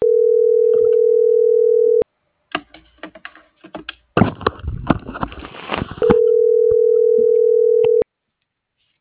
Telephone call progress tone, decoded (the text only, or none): ringback tone